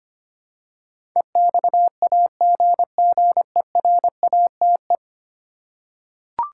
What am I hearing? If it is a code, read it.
Morse code, heard as a keyed tone, about 25 wpm, 700 Hz. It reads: EXAGGERATE
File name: 10 Letter Words - First 3 Letters Given - 25wpm